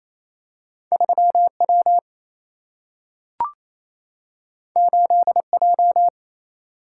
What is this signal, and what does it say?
Morse code, heard as a keyed tone, about 28 wpm, 700 Hz. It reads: 3 W 8 J
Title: Single-Letter Single-Number 28wpm